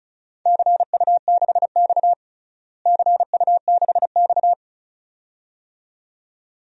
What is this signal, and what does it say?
Morse code, heard as a keyed tone, about 36 wpm, 700 Hz. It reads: CU6X CU6X
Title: DXCC-100 Call Signs - 4 Characters - Repeated 2x - 35wpm